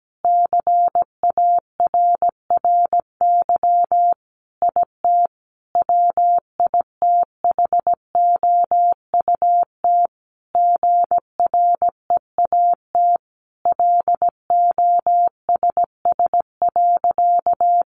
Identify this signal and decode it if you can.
Morse code, heard as a keyed tone, about 17 wpm, 700 Hz. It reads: CARRY IT WITHOUT GREAT LOSS.
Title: Sentences from Top 1400 Words 17wpm